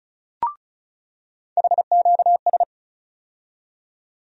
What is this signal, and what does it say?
Morse code, heard as a keyed tone, about 34 wpm, 700 Hz. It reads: HQS
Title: USA Call Sign Postfixes - 3 Letters 35wpm